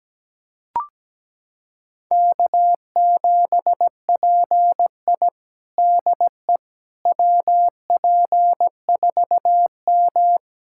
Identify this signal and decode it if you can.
Morse code, heard as a keyed tone, about 17 wpm, 700 Hz. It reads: K7PI DE WP4M
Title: USA Callsign DE Callsign - 4 Characters - 17wpm